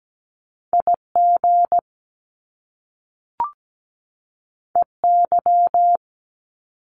Morse code, heard as a keyed tone, about 17 wpm, 700 Hz. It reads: IG EY
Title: Top Two Letter Combinations in Words 17wpm